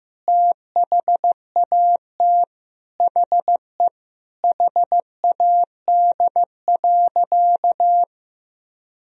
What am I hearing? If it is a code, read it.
Morse code, heard as a keyed tone, about 15 wpm, 700 Hz. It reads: THAT HE HAD.
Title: Sentences from Top 1000 Words - Review Part 2 - 15wpm